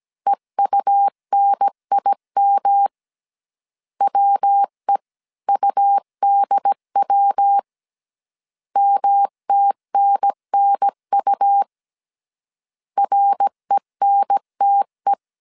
Morse code, heard as a keyed tone, about 17 wpm, 800 Hz. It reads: EUNIM WEUDW MTNNU RENTE